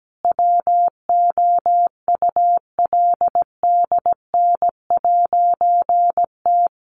Morse code, heard as a keyed tone, about 17 wpm, 700 Hz. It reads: WOULDN'T
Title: A Sound Of Thunder 17wpm